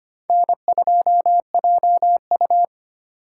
Morse code, heard as a keyed tone, about 31 wpm, 700 Hz. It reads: N2JU
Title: Speed Racing - USA 1x2 Call Signs 25wpm